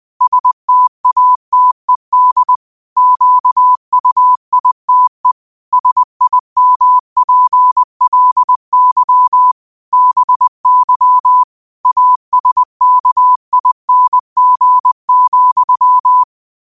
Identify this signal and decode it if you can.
Morse code, heard as a keyed tone, about 20 wpm, 1000 Hz. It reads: STATED QUITE SIMPLY BY ASKING,
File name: qotdfiles / 20wpm